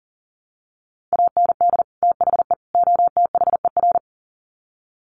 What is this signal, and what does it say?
Morse code, heard as a keyed tone, about 40 wpm, 700 Hz. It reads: AND THE OTHER
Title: Sets of 3 Words 40wpm